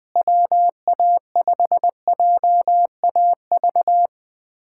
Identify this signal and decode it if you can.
Morse code, heard as a keyed tone, about 20 wpm, 700 Hz. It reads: WA5JAV
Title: USA Callsign DE Callsign 20wpm